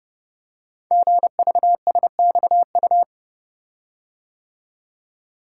Morse code, heard as a keyed tone, about 30 wpm, 700 Hz. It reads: GVSXU
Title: Code Groups - 5 Letters - 30wpm